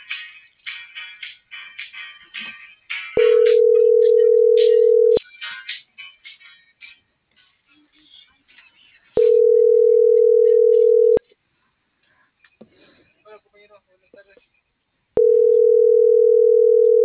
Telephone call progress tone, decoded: ringback tone

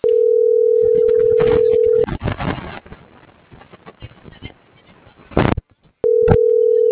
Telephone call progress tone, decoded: ringback tone

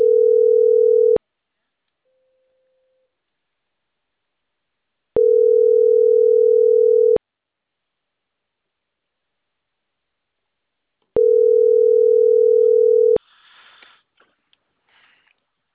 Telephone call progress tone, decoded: ringback tone

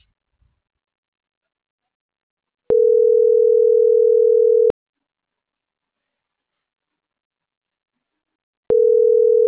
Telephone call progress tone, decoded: ringback tone

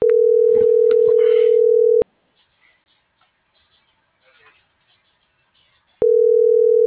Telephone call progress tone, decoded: ringback tone

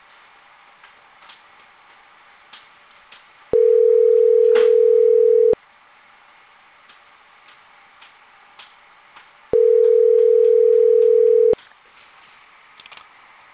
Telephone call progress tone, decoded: ringback tone

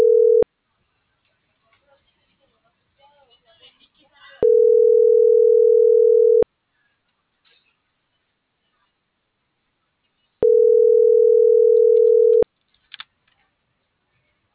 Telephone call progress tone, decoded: ringback tone

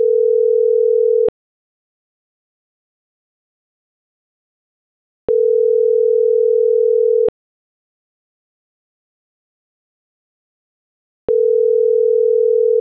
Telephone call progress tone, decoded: ringback tone

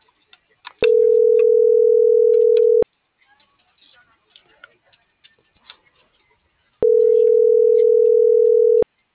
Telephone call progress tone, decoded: ringback tone

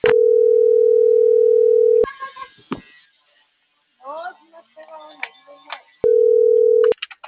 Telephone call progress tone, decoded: ringback tone

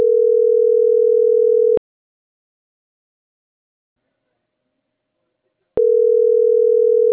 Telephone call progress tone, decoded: ringback tone

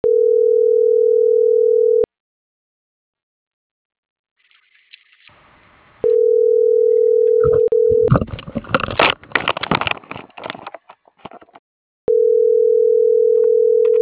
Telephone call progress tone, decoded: ringback tone

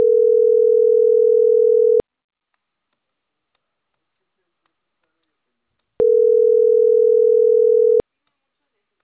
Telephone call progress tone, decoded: ringback tone

